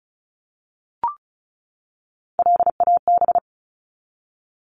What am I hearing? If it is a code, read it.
Morse code, heard as a keyed tone, about 34 wpm, 700 Hz. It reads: LAB
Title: Top 1751-2000 Words 35wpm